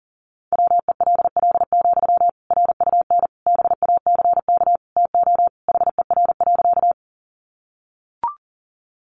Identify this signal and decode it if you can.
Morse code, heard as a keyed tone, about 40 wpm, 700 Hz. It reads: WELL, RUN BACK TO HER.
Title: Sentences from Top 300 Words 40wpm